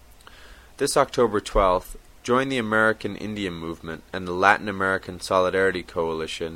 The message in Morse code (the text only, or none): none